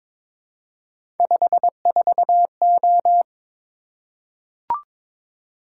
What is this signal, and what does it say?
Morse code, heard as a keyed tone, about 22 wpm, 700 Hz. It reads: 54O
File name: practice / 22wpm / Three Letter-Number Combinations - 22wpm